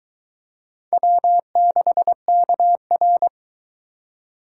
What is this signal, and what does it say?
Morse code, heard as a keyed tone, about 22 wpm, 700 Hz. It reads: W6KR